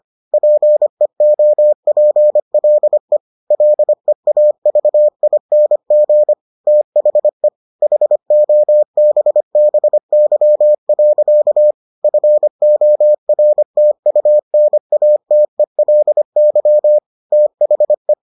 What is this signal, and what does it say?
Morse code, heard as a keyed tone, about 25 wpm, 600 Hz. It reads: PEOPLE LEAVING THE HOBBY. FORTUNATELY THE